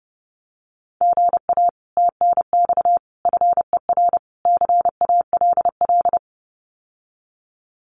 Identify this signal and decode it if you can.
Morse code, heard as a keyed tone, about 30 wpm, 700 Hz. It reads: GA TNX FER CALL
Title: Common QSO Phrases Encore 30wpm